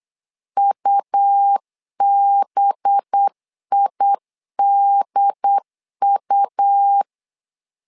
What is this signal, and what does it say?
Morse code, heard as a keyed tone, about 8 wpm, 800 Hz. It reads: UBIDU
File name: LEZIONI / 5 LBVC / 5 LB 8wpm 2816